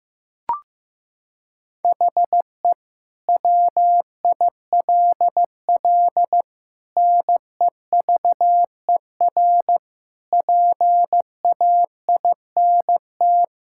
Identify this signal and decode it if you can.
Morse code, heard as a keyed tone, about 15 wpm, 700 Hz. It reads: HE WILL NEVER PAINT
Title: Sentences from Top 1000 Words - Review Part 6 - 15wpm